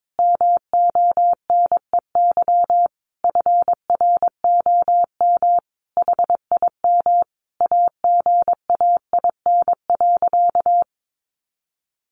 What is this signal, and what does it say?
Morse code, heard as a keyed tone, about 22 wpm, 700 Hz. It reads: MONEY FROM HIM AGAIN.
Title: Sentences from Top 1300 Words 22wpm